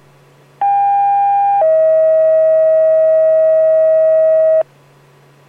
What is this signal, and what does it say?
Two-tone page, heard as a one-second tone, then a three-second tone, about 800 then 640 Hz. It is heard from 0.6 to 4.6 s.